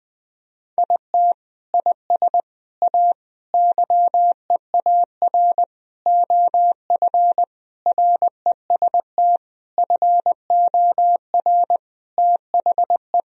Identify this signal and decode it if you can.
Morse code, heard as a keyed tone, about 20 wpm, 700 Hz. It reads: IT IS A YEAR OF REST FOR THE
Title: Sentences from Top 1000 Words - Review Part 1 - 20wpm